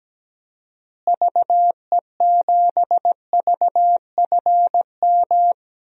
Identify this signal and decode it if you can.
Morse code, heard as a keyed tone, about 17 wpm, 700 Hz. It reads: VE7VFM
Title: DXCC-100 Call Signs - 6 Characters - Repeated 3x - 17wpm